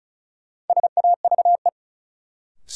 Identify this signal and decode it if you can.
Morse code, heard as a keyed tone, about 36 wpm, 700 Hz. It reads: SAVE
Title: Sentences from Top 1100 Words - New Word Drill Alt - 35wpm